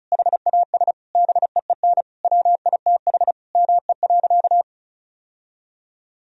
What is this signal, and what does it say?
Morse code, heard as a keyed tone, about 36 wpm, 700 Hz. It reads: HAS BEEN WITH ME.
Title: Sentences from Top 1000 Words - Review Part 1 - 35wpm